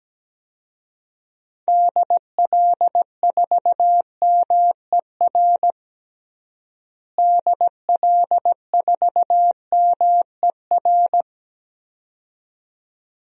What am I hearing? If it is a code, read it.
Morse code, heard as a keyed tone, about 17 wpm, 700 Hz. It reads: DL4MER DL4MER
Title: DXCC-100 Call Signs - 6 Characters - Repeated 2x - 17wpm